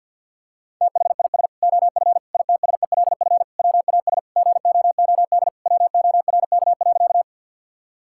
Morse code, heard as a keyed tone, about 50 wpm, 700 Hz. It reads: THIS OF ITSELF WAS GOOD WORK.
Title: Sentences from Top 1000 Words - Review Part 5 - 50wpm